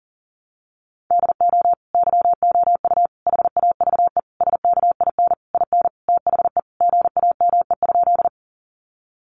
Morse code, heard as a keyed tone, about 40 wpm, 700 Hz. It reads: DO YOU HAVE SKIN IN THE GAME?
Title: Sentences from Top 1300 Words 40wpm